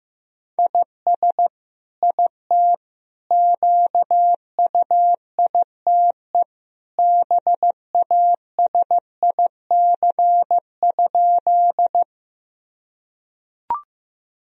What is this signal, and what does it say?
Morse code, heard as a keyed tone, about 15 wpm, 700 Hz. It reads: IS IT QUITE BASIC?
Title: Sentences from Top 1000 Words - Review Part 3 - 15wpm